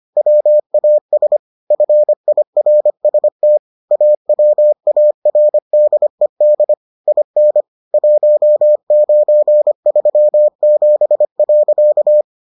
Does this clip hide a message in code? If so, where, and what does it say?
Yes: Morse code from 0.2 to 12.2 s, heard as a keyed tone, about 25 wpm, 600 Hz. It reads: WAS FIRST AWARDED IN 1937.